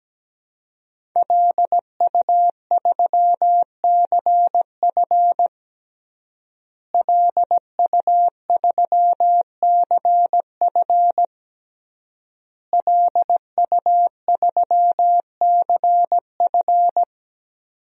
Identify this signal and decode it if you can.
Morse code, heard as a keyed tone, about 17 wpm, 700 Hz. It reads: LU3CF LU3CF LU3CF
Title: DXCC-100 Call Signs - 5 Characters - Repeated 3x - 17wpm